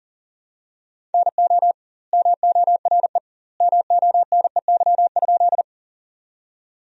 Morse code, heard as a keyed tone, about 40 wpm, 700 Hz. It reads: NO MORE MONEY?
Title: 2x Word-Spacing - Sentences from Top 400 Words - 40wpm